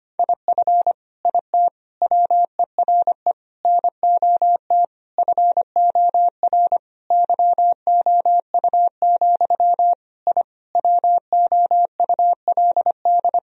Morse code, heard as a keyed tone, about 25 wpm, 700 Hz. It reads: IF IT WERE NOT FOR YOU, I WOULD
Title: Sentences from Top 1000 Words - Review Part 2 - 25wpm